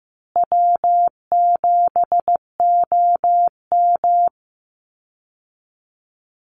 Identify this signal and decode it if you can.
Morse code, heard as a keyed tone, about 15 wpm, 700 Hz. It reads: W7OM
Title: Easy Call Signs Repeated 2x - 15wpm